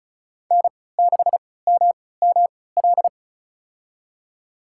Morse code, heard as a keyed tone, about 34 wpm, spaced as for 20 wpm, 700 Hz. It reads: N6MML
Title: USA Call Signs - 5 Characters - 3x Character Spacing - 35wpm